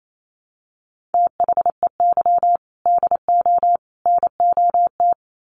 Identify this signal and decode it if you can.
Morse code, heard as a keyed tone, about 28 wpm, 700 Hz. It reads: THEY DO NOT
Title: Sets of 3 Words 28wpm